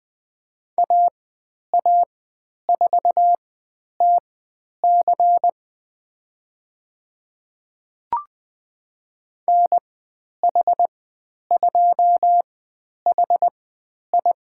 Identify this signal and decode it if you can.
Morse code, heard as a keyed tone, about 20 wpm, spaced as for 10 wpm, 700 Hz. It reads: AA4TC NH2HI
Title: 109 Lesson 18g - Letter m - Call Signs 20wpm